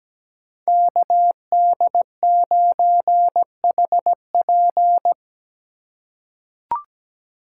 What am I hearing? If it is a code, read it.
Morse code, heard as a keyed tone, about 17 wpm, 700 Hz. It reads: KD9HP
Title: USA Call Signs - 2x Character Spacing - 17wpm